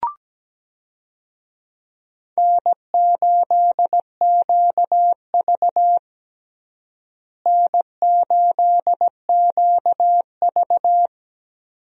Morse code, heard as a keyed tone, about 17 wpm, 700 Hz. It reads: N8QV N8QV
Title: USA Call Signs - 4 Characters - Repeated 2x - 17wpm